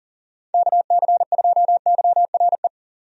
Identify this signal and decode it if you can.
Morse code, heard as a keyed tone, about 40 wpm, 700 Hz. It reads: KC2YRE
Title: USA Call Signs - 6 Characters - 40wpm